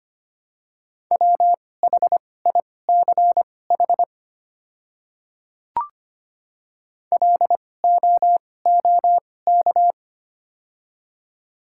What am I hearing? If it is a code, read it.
Morse code, heard as a keyed tone, about 25 wpm, spaced as for 18 wpm, 700 Hz. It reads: WHICH LOOK
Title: 2x Character Spacing - Top 200 Words - 25wpm